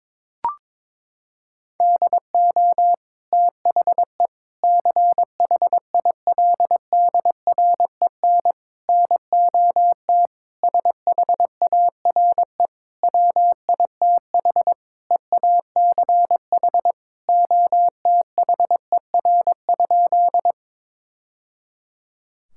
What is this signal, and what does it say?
Morse code, heard as a keyed tone, about 22 wpm, 700 Hz. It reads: DO THE CHILDREN NOT SHARE WITH EACH OTHER?
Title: Sentences from Top 1000 Words - Review Part 6 - 22wpm